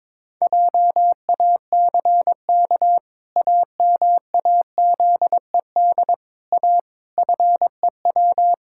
Morse code, heard as a keyed tone, about 22 wpm, 700 Hz. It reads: JACK AMAZED A FEW
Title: Pangram Challenge v2 - 22wpm